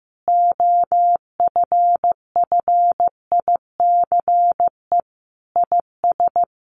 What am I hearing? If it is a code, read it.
Morse code, heard as a keyed tone, about 15 wpm, 700 Hz. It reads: OFFICE IS
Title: Sentences from Top 1000 Words 15wpm